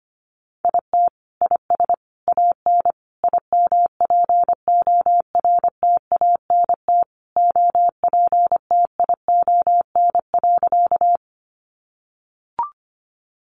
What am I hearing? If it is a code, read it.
Morse code, heard as a keyed tone, about 25 wpm, 700 Hz. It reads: IT IS AN IMPORTANT OPTION.
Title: Sentences from Top 1300 Words 25wpm